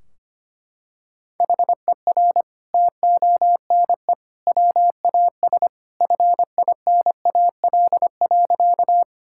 Morse code, heard as a keyed tone, about 25 wpm, 700 Hz. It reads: HER TONE WAS FINAL.
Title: Sentences from Top 1600 Words 25wpm